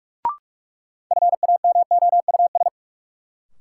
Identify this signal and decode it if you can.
Morse code, heard as a keyed tone, about 46 wpm, 700 Hz. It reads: FAMOUS
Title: Sentences from Top 1500 Words - New Word Drill - 45wpm